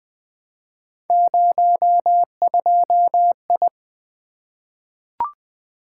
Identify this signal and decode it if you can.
Morse code, heard as a keyed tone, about 21 wpm, 700 Hz. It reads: 02I
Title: Three Letter-Number Combinations - 20wpm